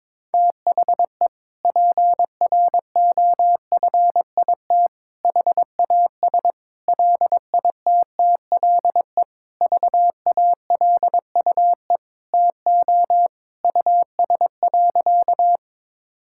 Morse code, heard as a keyed tone, about 22 wpm, 700 Hz. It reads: THE PROFIT HAS LITTLE VALUE TO US.
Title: Sentences from Top 1000 Words - Review Part 5 - 22wpm